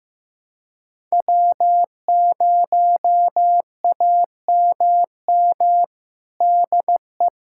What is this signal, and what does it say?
Morse code, heard as a keyed tone, about 15 wpm, 700 Hz. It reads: W0AMM DE
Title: USA Callsign DE Callsign 15wpm